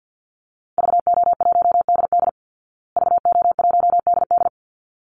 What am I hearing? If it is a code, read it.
Morse code, heard as a keyed tone, about 49 wpm, 700 Hz. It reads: VO1DD VO1DD
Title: Call Signs Repeated 3x - 50wpm